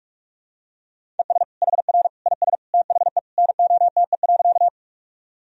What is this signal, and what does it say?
Morse code, heard as a keyed tone, about 41 wpm, 700 Hz. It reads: ES HR IS THE NOTE.
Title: Sentences from Top 500 Words and Abbreviations 45wpm